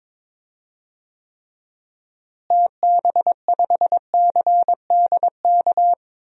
Morse code, heard as a keyed tone, about 22 wpm, 700 Hz. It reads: TB5CDK
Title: DXCC-100 Call Signs - 6 Characters - Repeated 2x - 22wpm